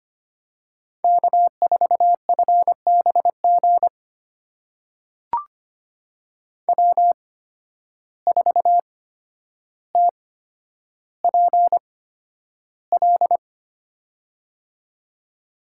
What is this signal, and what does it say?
Morse code, heard as a keyed tone, about 25 wpm, spaced as for 11 wpm, 700 Hz. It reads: K4FBG W4TPL
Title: USA Call Signs - 8x Character Spacing - 25wpm